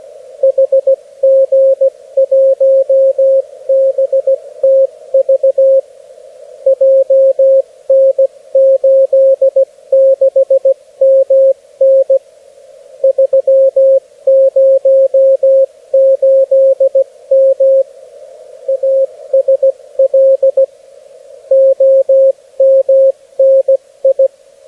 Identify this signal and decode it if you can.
Morse code, heard as a keyed tone, about 17 wpm, 540 Hz, with radio noise. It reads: HG1BTV JN86MN 308M ASL OMNI
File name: beacons / HG1BTV